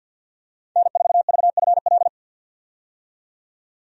Morse code, heard as a keyed tone, about 50 wpm, 700 Hz. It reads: N4VFL
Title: USA Call Signs - 5 Characters - 50wpm